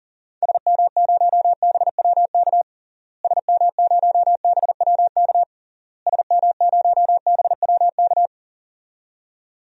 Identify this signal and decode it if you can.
Morse code, heard as a keyed tone, about 40 wpm, 700 Hz. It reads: SM0BWK SM0BWK SM0BWK
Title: DXCC-100 Call Signs - 6 Characters - Repeated 3x - 40wpm